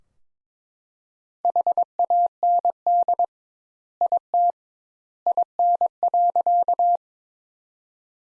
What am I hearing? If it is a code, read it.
Morse code, heard as a keyed tone, about 22 wpm, 700 Hz. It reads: HAND IT IN.
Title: Sentences Top 200 Words - 22wpm